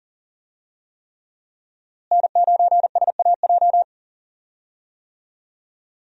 Morse code, heard as a keyed tone, about 40 wpm, 700 Hz. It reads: N9SAJ
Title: USA Call Signs - 5 Characters - Repeated 3x - 40wpm